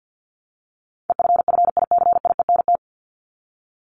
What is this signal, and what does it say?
Morse code, heard as a keyed tone, about 50 wpm, 700 Hz. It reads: EFFICIENT
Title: Top 1251-1500 Words 50wpm